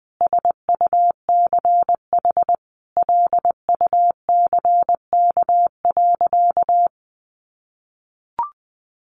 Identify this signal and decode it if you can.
Morse code, heard as a keyed tone, about 20 wpm, 700 Hz. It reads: SUCH LUCK.
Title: Sentences from Top 1500 Words 20wpm